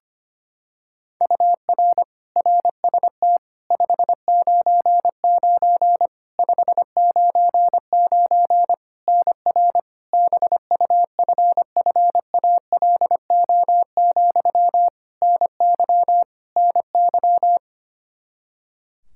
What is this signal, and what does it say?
Morse code, heard as a keyed tone, about 25 wpm, 700 Hz. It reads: UR RST 599 599 NR BUFFALO, NY NY
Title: K1USN SST QSO Practice 25wpm